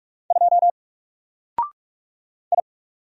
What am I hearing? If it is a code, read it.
Morse code, heard as a keyed tone, about 46 wpm, 700 Hz. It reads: 2 I
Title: Single Character (Letter:Number:Punctuation:Prosign) Alt 45wpm